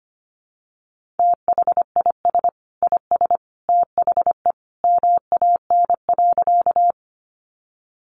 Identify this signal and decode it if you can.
Morse code, heard as a keyed tone, about 25 wpm, 700 Hz. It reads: THIS IS THE MAN.
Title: Sentences from Top 200 Words 25wpm